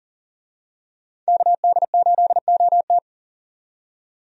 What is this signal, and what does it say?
Morse code, heard as a keyed tone, about 40 wpm, 700 Hz. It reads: KD8OT
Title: USA Call Signs - 5 Characters - 40wpm